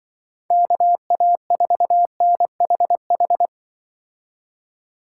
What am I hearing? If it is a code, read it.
Morse code, heard as a keyed tone, about 25 wpm, 700 Hz. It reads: KA4NHH
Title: Speed Racing - USA 2x3 Call Signs 20wpm